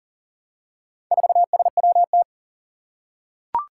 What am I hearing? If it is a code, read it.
Morse code, heard as a keyed tone, about 40 wpm, 700 Hz. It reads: 4SWT
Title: Four Challenging Letter-Number Combinations 40wpm